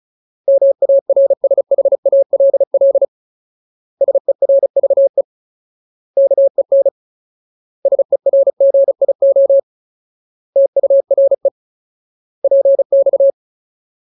Morse code, heard as a keyed tone, about 35 wpm, 550 Hz. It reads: MARSHALL SERVE KEN SERGIO TURE PX
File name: Ultimate Word List / 35WPM Ultimate